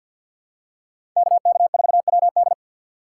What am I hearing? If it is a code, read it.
Morse code, heard as a keyed tone, about 46 wpm, 700 Hz. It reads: KK4WD
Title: USA Call Signs - 5 Characters - 4x Character Spacing - 50wpm